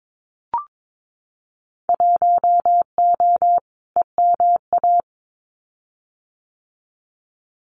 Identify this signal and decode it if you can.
Morse code, heard as a keyed tone, about 22 wpm, 700 Hz. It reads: 1O EMA
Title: Winter Field Day Exchanges Encore 22wpm